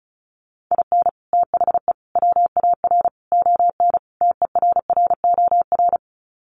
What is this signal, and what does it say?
Morse code, heard as a keyed tone, about 34 wpm, 700 Hz. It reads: IN THE WAR ON TERROR
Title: Sets of 5 Words Encore2 35wpm